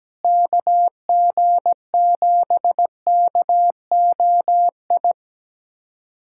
Morse code, heard as a keyed tone, about 17 wpm, 700 Hz. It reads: KG7KOI
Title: USA Callsign DE Callsign - 6 Characters - 17wpm